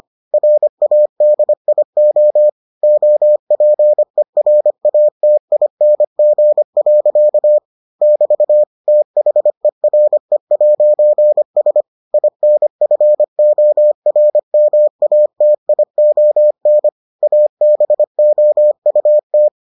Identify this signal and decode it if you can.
Morse code, heard as a keyed tone, about 25 wpm, 600 Hz. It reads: RADIO OPERATING. <BT> THERE'S INFORMATION ABOUT